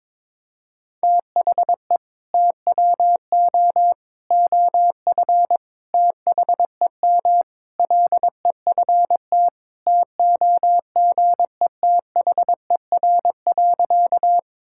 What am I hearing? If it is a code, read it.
Morse code, heard as a keyed tone, about 22 wpm, 700 Hz. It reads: THE TWO OF THEM LEFT TOGETHER.
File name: practice / 22wpm / Sentences with 4-Word Phrases - 22wpm